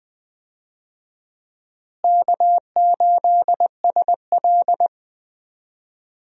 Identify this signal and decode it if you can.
Morse code, heard as a keyed tone, about 21 wpm, 700 Hz. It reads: K8SL
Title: USA Call Signs - 4 Characters - Repeated 2x - 20wpm